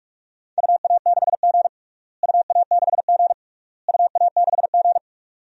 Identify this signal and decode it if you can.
Morse code, heard as a keyed tone, about 46 wpm, 700 Hz. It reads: UA6G UA6G UA6G
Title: DXCC-100 Call Signs - 4 Characters - Repeated 3x - 45wpm